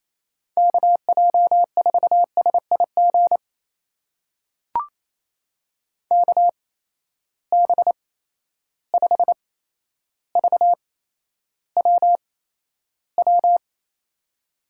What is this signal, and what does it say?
Morse code, heard as a keyed tone, about 28 wpm, spaced as for 12 wpm, 700 Hz. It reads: KJ4SIG KB5VWW
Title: USA Call Signs - 6 Characters - 8x Character Spacing - 28wpm